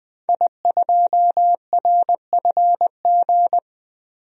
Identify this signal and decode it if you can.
Morse code, heard as a keyed tone, about 20 wpm, 700 Hz. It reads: I2RFG
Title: DXCC-100 Call Signs - 5 Characters - 20wpm